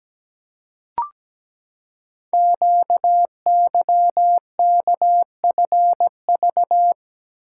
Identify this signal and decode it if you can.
Morse code, heard as a keyed tone, about 17 wpm, 700 Hz. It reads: QYKFV
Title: Code Groups - 5 Letters - 17wpm